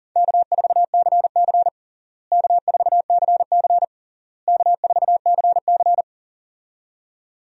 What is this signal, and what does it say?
Morse code, heard as a keyed tone, about 40 wpm, 700 Hz. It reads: K4CC K4CC K4CC
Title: USA Call Signs - 4 Characters - Repeated 3x - 40wpm